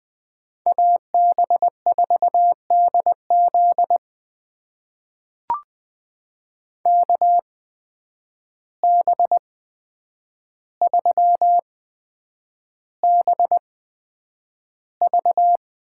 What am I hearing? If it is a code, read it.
Morse code, heard as a keyed tone, about 20 wpm, spaced as for 9 wpm, 700 Hz. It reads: AB4DZ KB3BV
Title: USA Call Signs - 5 Characters - 8x Character Spacing - 20wpm